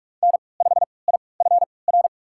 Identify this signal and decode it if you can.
Morse code, heard as a keyed tone, about 46 wpm, spaced as for 25 wpm, 700 Hz. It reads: N5IFR
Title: USA Call Signs - 5 Characters - 3x Character Spacing - 45wpm